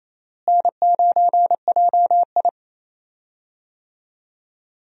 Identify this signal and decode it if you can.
Morse code, heard as a keyed tone, about 28 wpm, 700 Hz. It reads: N9JI
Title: USA Call Signs - 4 Characters - Repeated 3x - 28wpm